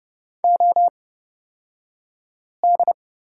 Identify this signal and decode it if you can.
Morse code, heard as a keyed tone, about 31 wpm, spaced as for 5 wpm, 700 Hz. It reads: OD
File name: Lesson 37d - Letter x - Two Characters 30wpm